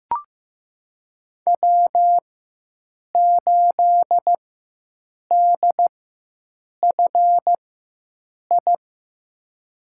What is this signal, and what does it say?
Morse code, heard as a keyed tone, about 15 wpm, spaced as for 7 wpm, 700 Hz. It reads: W8DFI
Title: USA Call Signs - 5 Characters - 4x Character Spacing - 15wpm